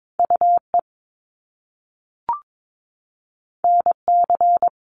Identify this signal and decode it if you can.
Morse code, heard as a keyed tone, about 22 wpm, 700 Hz. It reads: UE NC
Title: Top Two Letter Combinations in Words 22wpm